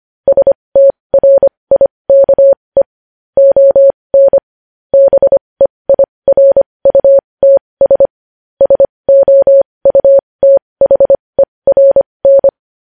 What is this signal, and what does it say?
Morse code, heard as a keyed tone, about 25 wpm, spaced as for 20 wpm, 550 Hz. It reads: STRIKE ON BEIRUTS SOUTHERN
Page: news headlines in morse code at 20 wpm